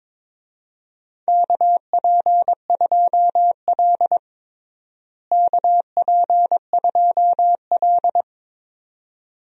Morse code, heard as a keyed tone, about 22 wpm, 700 Hz. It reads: KP2L KP2L
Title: DXCC-100 Call Signs - 4 Characters - Repeated 2x - 22wpm